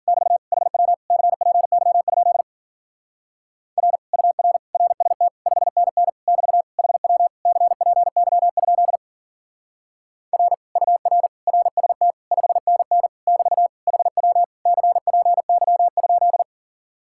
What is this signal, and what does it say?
Morse code, heard as a keyed tone, about 51 wpm, 700 Hz. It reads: <BT> HW CPY? R UR RST 5NN <BT> HW CPY? R UR RST 5NN <BT> HW CPY?